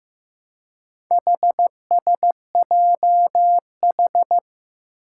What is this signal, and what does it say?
Morse code, heard as a keyed tone, about 15 wpm, 700 Hz. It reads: HSJH
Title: Four Challenging Letter-Number Combinations 15wpm